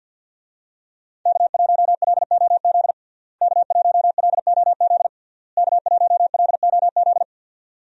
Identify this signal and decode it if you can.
Morse code, heard as a keyed tone, about 50 wpm, 700 Hz. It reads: K1LOZ K1LOZ K1LOZ
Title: USA Call Signs - 5 Characters - Repeated 3x - 50wpm